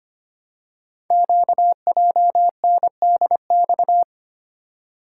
Morse code, heard as a keyed tone, about 25 wpm, 700 Hz. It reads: QJNDX
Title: Code Groups - 5 Letters - 25wpm